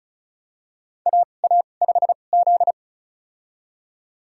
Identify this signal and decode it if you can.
Morse code, heard as a keyed tone, about 36 wpm, spaced as for 25 wpm, 700 Hz. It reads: AA5Z
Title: USA Call Signs - 4 Characters - 2x Character Spacing - 35wpm